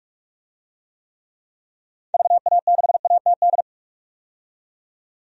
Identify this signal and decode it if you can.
Morse code, heard as a keyed tone, about 46 wpm, 700 Hz. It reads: VA6ATD
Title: DXCC-100 Call Signs - 6 Characters - 45wpm